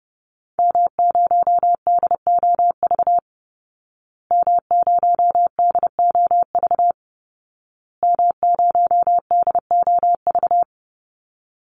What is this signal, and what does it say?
Morse code, heard as a keyed tone, about 30 wpm, 700 Hz. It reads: M0DOV M0DOV M0DOV
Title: Call Signs Repeated 2x - 30wpm